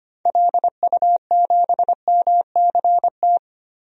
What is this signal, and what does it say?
Morse code, heard as a keyed tone, about 25 wpm, 690 Hz. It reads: LU7MCT
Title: DXCC-100 Call Signs - 6 Characters - 25wpm